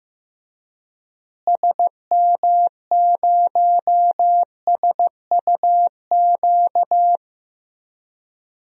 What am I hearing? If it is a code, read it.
Morse code, heard as a keyed tone, about 15 wpm, 700 Hz. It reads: SM0SUQ
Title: DXCC-100 Call Signs - 6 Characters - Repeated 3x - 15wpm